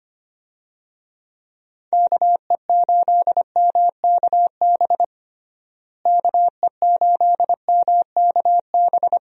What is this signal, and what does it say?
Morse code, heard as a keyed tone, about 25 wpm, 700 Hz. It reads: KE8MKB KE8MKB
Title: USA Call Signs - 6 Characters - Repeated 2x - 25wpm